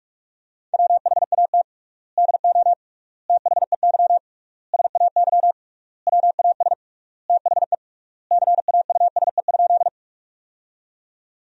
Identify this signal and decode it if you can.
Morse code, heard as a keyed tone, about 46 wpm, 700 Hz. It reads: WHAT DO THEY SAY WAS THE CAUSE?
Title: 3x Word-Spacing - Sentences from Top 200 Words - 45wpm